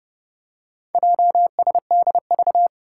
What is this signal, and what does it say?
Morse code, heard as a keyed tone, about 31 wpm, 700 Hz. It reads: JSDV
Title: Four Challenging Letter-Number Combinations 30wpm